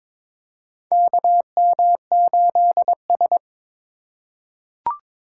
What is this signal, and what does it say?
Morse code, heard as a keyed tone, about 22 wpm, 700 Hz. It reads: KM8S
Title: USA Call Signs - 4 Characters - 3x Character Spacing - 22wpm